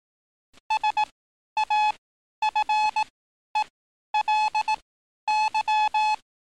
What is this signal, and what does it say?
Morse code, heard as a keyed tone, about 18 wpm, spaced as for 11 wpm, 860 Hz. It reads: SAFELY
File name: training / NAVPERS 11049to11060 / NAVPERS 11051 RA RB